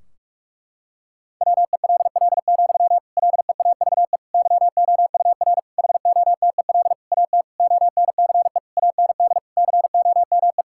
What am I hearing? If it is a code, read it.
Morse code, heard as a keyed tone, about 46 wpm, 700 Hz. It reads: WELL, LEAVE YOUR HOTEL AT ONCE AND COME
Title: Sentences from Top 1600 Words 45wpm